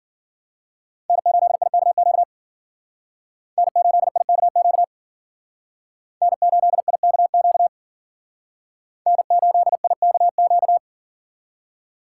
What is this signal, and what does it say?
Morse code, heard as a keyed tone, about 51 wpm, 700 Hz. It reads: N8IKQ N8IKQ N8IKQ N8IKQ